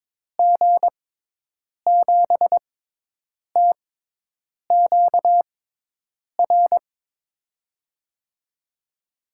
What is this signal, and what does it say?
Morse code, heard as a keyed tone, about 22 wpm, spaced as for 8 wpm, 700 Hz. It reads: G7TQR
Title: USA Call Signs - 6 Characters - 6x Character Spacing - 22wpm